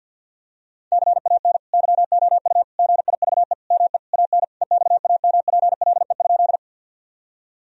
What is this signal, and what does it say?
Morse code, heard as a keyed tone, about 52 wpm, 700 Hz. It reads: CAN YOU GIVE ME AN EXAMPLE?
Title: Sentences from Top 1000 Words - Review Part 1 - 50wpm